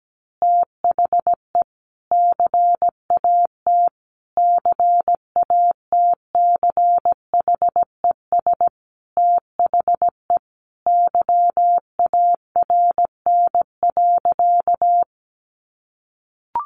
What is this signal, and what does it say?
Morse code, heard as a keyed tone, about 17 wpm, 700 Hz. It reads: THE CAT CATCHES THE YARN.
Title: Additional Easy Phrases 17wpm